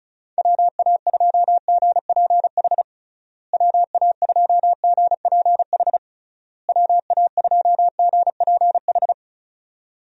Morse code, heard as a keyed tone, about 34 wpm, 700 Hz. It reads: WA2GPH WA2GPH WA2GPH
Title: USA Call Signs - 6 Characters - Repeated 3x - 35wpm